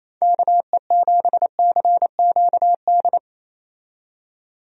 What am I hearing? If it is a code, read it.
Morse code, heard as a keyed tone, about 28 wpm, 700 Hz. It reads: KE7CQD
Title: USA Call Signs - 6 Characters - 28wpm